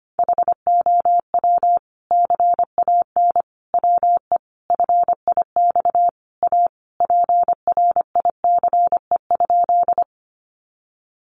Morse code, heard as a keyed tone, about 25 wpm, 700 Hz. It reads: HOW CAN WE FIX A PRICE?
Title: Sentences from Top 1400 Words 25wpm